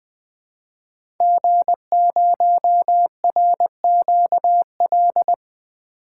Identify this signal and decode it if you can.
Morse code, heard as a keyed tone, about 20 wpm, 700 Hz. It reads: G0RQL
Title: 2023 SOTA Chaser Call Signs - 20wpm